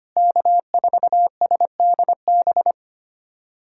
Morse code, heard as a keyed tone, about 25 wpm, 700 Hz. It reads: K4SDB
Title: USA Callsign DE Callsign - 5 Characters - 25wpm